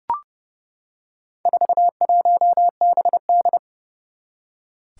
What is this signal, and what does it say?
Morse code, heard as a keyed tone, about 32 wpm, 700 Hz. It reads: 41BD
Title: Four Challenging Letter-Number Combinations 30wpm